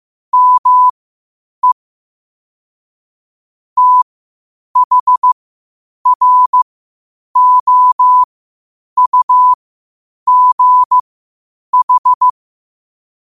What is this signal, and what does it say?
Morse code, heard as a keyed tone, about 15 wpm, spaced as for 8 wpm, 1000 Hz. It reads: ME THROUGH
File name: qotdfiles / 8wpm